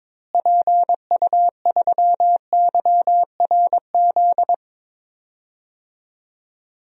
Morse code, heard as a keyed tone, about 22 wpm, 700 Hz. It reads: PU3YRZ
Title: DXCC-100 Call Signs - 6 Characters - Repeated 3x - 22wpm